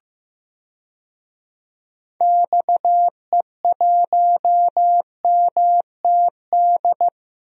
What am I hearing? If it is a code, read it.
Morse code, heard as a keyed tone, about 15 wpm, 700 Hz. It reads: XE1MTD
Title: DXCC-100 Call Signs - 6 Characters - Repeated 2x - 15wpm